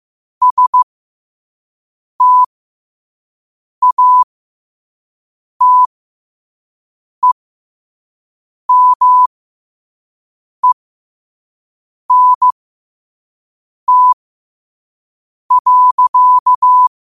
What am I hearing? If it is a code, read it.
Morse code, heard as a keyed tone, about 15 wpm, spaced as for 5 wpm, 1000 Hz. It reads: STATEMENT.
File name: 5wpm